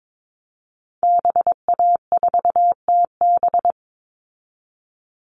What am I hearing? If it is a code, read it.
Morse code, heard as a keyed tone, about 22 wpm, 700 Hz. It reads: BA4TB
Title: Call Signs Repeated 2x - 22wpm